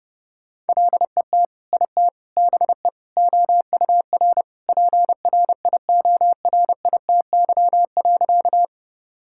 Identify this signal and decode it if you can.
Morse code, heard as a keyed tone, about 30 wpm, 700 Hz. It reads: LET IT BE OUR PRIORITY.
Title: Sentences from Top 1600 Words 30wpm